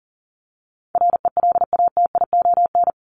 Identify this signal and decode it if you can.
Morse code, heard as a keyed tone, about 40 wpm, 700 Hz. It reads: RELATION
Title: Top 1751-2000 Words 40wpm